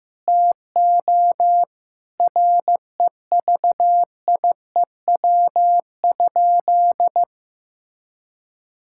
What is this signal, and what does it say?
Morse code, heard as a keyed tone, about 16 wpm, 700 Hz. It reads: TO REVIEW?
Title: Sentences from Top 1000 Words - Review Part 6 - 15wpm